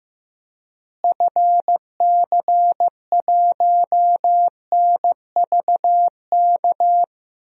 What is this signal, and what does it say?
Morse code, heard as a keyed tone, about 16 wpm, 700 Hz. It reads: FC1NVK
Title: DXCC-100 Call Signs - 6 Characters - Repeated 3x - 15wpm